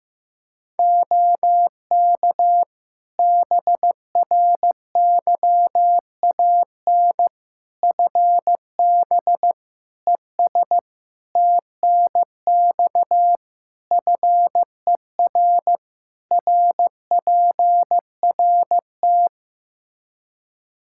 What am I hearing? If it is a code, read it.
Morse code, heard as a keyed tone, about 15 wpm, 700 Hz. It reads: OK BRYAN FB ES TNX FER RPRT
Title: Practice QSO - Exchange 2 - Info - 15wpm